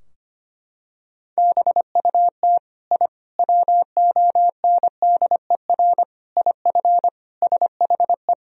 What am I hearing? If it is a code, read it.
Morse code, heard as a keyed tone, about 25 wpm, 700 Hz. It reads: BUT I WONDER IF SHE
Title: Sentences from Top 1000 Words - Review Part 3 - 25wpm